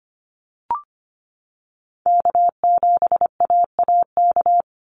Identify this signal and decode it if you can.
Morse code, heard as a keyed tone, about 25 wpm, 700 Hz. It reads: K7AAK
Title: USA Call Signs 25wpm